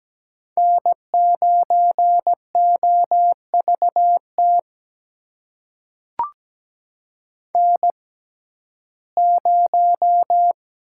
Morse code, heard as a keyed tone, about 17 wpm, spaced as for 12 wpm, 700 Hz. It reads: N9OVT N0
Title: USA Call Signs - 5 Characters - 6x Character Spacing - 17wpm